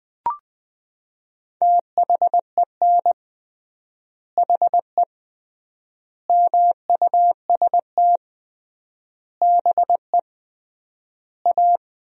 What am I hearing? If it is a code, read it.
Morse code, heard as a keyed tone, about 20 wpm, 700 Hz. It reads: THEN HE MUST BE A
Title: 3x Word-Spacing - Sentences from Top 500 Words - 20wpm